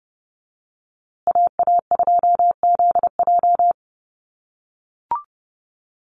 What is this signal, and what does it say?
Morse code, heard as a keyed tone, about 30 wpm, 700 Hz. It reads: AA2ZJ
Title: USA Call Signs Repeated 2x 30wpm